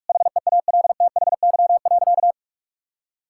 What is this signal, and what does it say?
Morse code, heard as a keyed tone, about 41 wpm, 700 Hz. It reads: HEALTHY.
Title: Sentences from Top 1000 Words - Review Part 5 - 45wpm